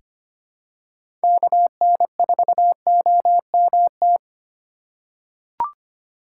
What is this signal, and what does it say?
Morse code, heard as a keyed tone, about 25 wpm, 700 Hz. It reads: KN4OMT
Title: USA Call Signs - 6 Characters - 2x Character Spacing - 25wpm